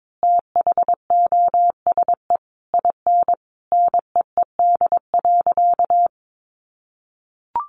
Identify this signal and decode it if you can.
Morse code, heard as a keyed tone, about 22 wpm, 700 Hz. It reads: THOSE IN NEED.
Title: Sentences from Top 1200 Words 22wpm